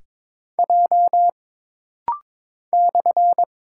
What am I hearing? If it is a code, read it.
Morse code, heard as a keyed tone, about 22 wpm, 700 Hz. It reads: J /
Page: This rapid-fire timing is geared to push your ICR (Instant Character Recognition) skills.